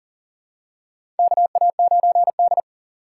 Morse code, heard as a keyed tone, about 39 wpm, 700 Hz. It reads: KA9D
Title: USA Call Signs - 4 Characters - 6x Character Spacing - 40wpm